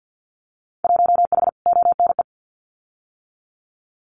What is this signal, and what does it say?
Morse code, heard as a keyed tone, about 50 wpm, 700 Hz. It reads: 1H ONE
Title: Winter Field Day Exchanges Encore 50wpm